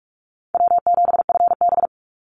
Morse code, heard as a keyed tone, about 44 wpm, 700 Hz. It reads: W7FB
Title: Easy Call Signs Repeated 3x - 45wpm